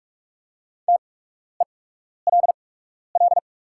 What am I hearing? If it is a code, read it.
Morse code, heard as a keyed tone, about 44 wpm, spaced as for 12 wpm, 690 Hz. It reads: TELL
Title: 8x Character Spacing - Top 200 Words - 45wpm